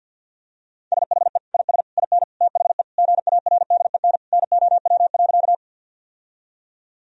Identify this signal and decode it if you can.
Morse code, heard as a keyed tone, about 48 wpm, 700 Hz. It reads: SHE IS IN THE GARDEN NOW.
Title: Sentences from Top 1000 Words - Review Part 4 - 50wpm